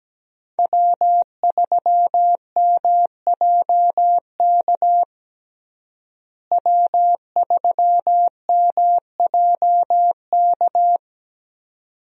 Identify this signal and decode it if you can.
Morse code, heard as a keyed tone, about 17 wpm, 700 Hz. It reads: W3MJK W3MJK
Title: USA Call Signs - 5 Characters - Repeated 2x - 17wpm